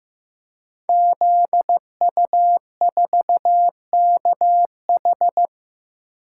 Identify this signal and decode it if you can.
Morse code, heard as a keyed tone, about 15 wpm, 700 Hz. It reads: ZU4KH
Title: Code Groups - 5 Letter:Numbers - 15wpm